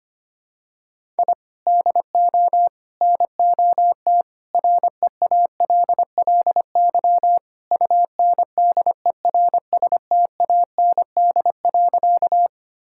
Morse code, heard as a keyed tone, about 25 wpm, 700 Hz. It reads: I DO NOT REALLY UNDERSTAND.
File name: practice / 25wpm / Sentences with 4-Word Phrases - 25wpm